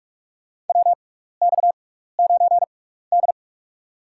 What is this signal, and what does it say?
Morse code, heard as a keyed tone, about 44 wpm, spaced as for 15 wpm, 700 Hz. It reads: WX9D
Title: USA Call Signs - 4 Characters - 6x Character Spacing - 45wpm